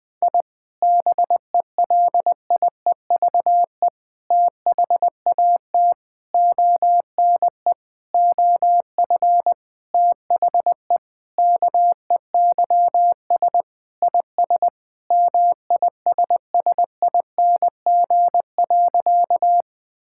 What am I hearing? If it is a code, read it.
Morse code, heard as a keyed tone, about 20 wpm, 700 Hz. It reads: I BELIEVE THAT ONE OF THE KEYS IS MISSING.
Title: Sentences with 4-Word Phrases - Spoken Beginning - 20wpm